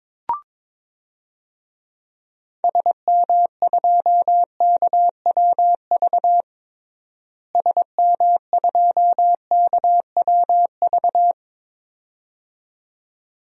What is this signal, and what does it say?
Morse code, heard as a keyed tone, about 22 wpm, 700 Hz. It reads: SM2KWV SM2KWV
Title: DXCC-100 Call Signs - 6 Characters - Repeated 2x - 22wpm